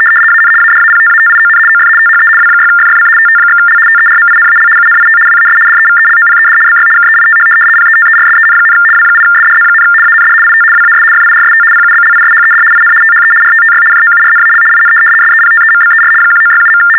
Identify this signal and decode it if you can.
RTTY, about 75 baud. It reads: AN ADAPTATION OF THE W6FFC TT/L TERMINAL UNIT WAS DEVELOPED BY KEITH PETERSEN, W8SDZ, AND IT WAS FIRST PUBLISHED IN THE RTTY JOURNAL IN SEPTEMBER 1967.
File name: teletype-RTTY-75